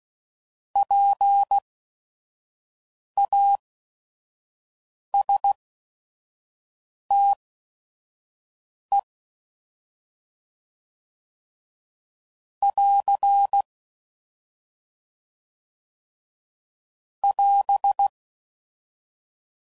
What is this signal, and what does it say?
Morse code, heard as a keyed tone, about 16 wpm, spaced as for 5 wpm, 800 Hz. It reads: PASTE <AR> <AS>
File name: Prefixes / By First Letter / Ps